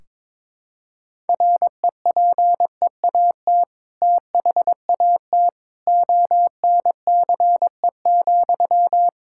Morse code, heard as a keyed tone, about 21 wpm, 700 Hz. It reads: REPEAT THAT ONCE,
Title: Sentences from Top 1600 Words 22wpm